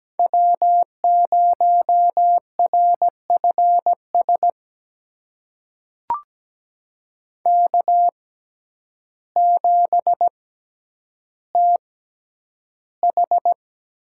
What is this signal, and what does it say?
Morse code, heard as a keyed tone, about 17 wpm, spaced as for 9 wpm, 700 Hz. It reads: W0RFS K7TH
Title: USA Call Signs - 5 Characters - 6x Character Spacing - 17wpm